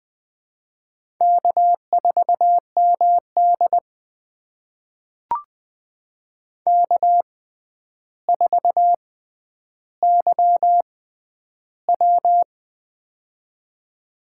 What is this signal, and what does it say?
Morse code, heard as a keyed tone, about 20 wpm, spaced as for 10 wpm, 700 Hz. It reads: K4MD K4YW
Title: USA Call Signs - 4 Characters - 6x Character Spacing - 20wpm